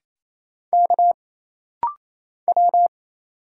Morse code, heard as a keyed tone, about 28 wpm, 700 Hz. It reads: K W
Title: Single Letter-Number - Rapid-Fire 28wpm